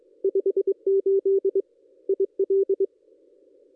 Morse code, heard as a keyed tone, about 22 wpm, 390 Hz. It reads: 58 IL
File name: contests / ss-cw-k5tr-op-n5rz -1800